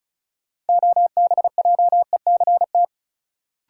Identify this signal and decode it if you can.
Morse code, heard as a keyed tone, about 36 wpm, 700 Hz. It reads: OBJECT
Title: Sentences from Top 500 Words - New Word Drill - 35wpm